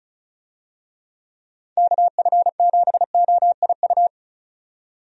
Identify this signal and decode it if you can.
Morse code, heard as a keyed tone, about 34 wpm, 700 Hz. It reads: KF7OIU
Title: USA Call Signs - 6 Characters - 35wpm